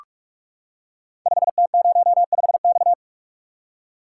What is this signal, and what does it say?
Morse code, heard as a keyed tone, about 44 wpm, 700 Hz. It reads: 5T05X
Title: Code Groups - 5 Letter:Numbers - 45wpm